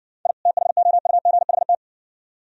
Morse code, heard as a keyed tone, about 60 wpm, 690 Hz. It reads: I THOUGHT